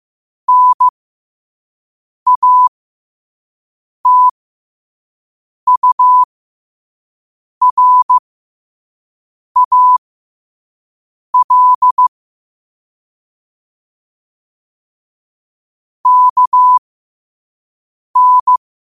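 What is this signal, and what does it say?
Morse code, heard as a keyed tone, about 15 wpm, spaced as for 5 wpm, 1000 Hz. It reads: NATURAL KN
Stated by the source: Morse code at 5 wpm.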